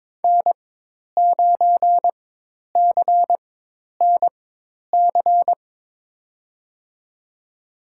Morse code, heard as a keyed tone, about 22 wpm, spaced as for 10 wpm, 700 Hz. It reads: N9CNC
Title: USA Call Signs - 5 Characters - 4x Character Spacing - 22wpm